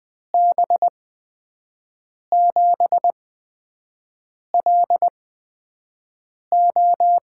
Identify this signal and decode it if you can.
Morse code, heard as a keyed tone, about 20 wpm, spaced as for 5 wpm, 700 Hz. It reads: B7LO
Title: USA Call Signs - 6 Characters - 8x Character Spacing - 20wpm